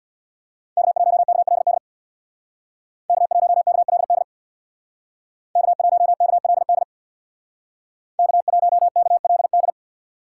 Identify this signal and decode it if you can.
Morse code, heard as a keyed tone, about 60 wpm, 700 Hz. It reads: K1KLD K1KLD K1KLD K1KLD
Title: Speed Racing - USA 1x3 Call Signs 50wpm